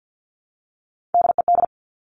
Morse code, heard as a keyed tone, about 50 wpm, 700 Hz. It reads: DED
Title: Top Three Letter Combinations in Words 50wpm